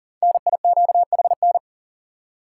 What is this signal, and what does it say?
Morse code, heard as a keyed tone, about 40 wpm, 700 Hz. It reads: NIQHN
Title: Code Groups - 5 Letter:Numbers - 40wpm